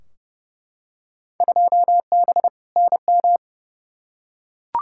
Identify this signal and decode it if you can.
Morse code, heard as a keyed tone, about 32 wpm, 700 Hz. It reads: 2B NM
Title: Field Day Exchanges Encore - 30wpm